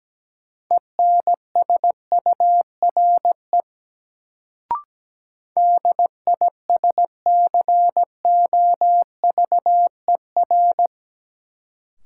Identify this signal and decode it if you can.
Morse code, heard as a keyed tone, about 17 wpm, 700 Hz. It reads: ENSURE DISCOVER
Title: Sentences from Top 1100 Words - New Word Drill Alt - 17wpm